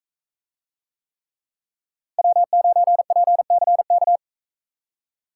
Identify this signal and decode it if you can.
Morse code, heard as a keyed tone, about 44 wpm, 700 Hz. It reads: W9PCK